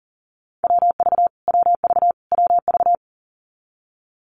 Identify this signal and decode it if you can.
Morse code, heard as a keyed tone, about 40 wpm, 700 Hz. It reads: WV WV WV
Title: US States Repeated 3x 40wpm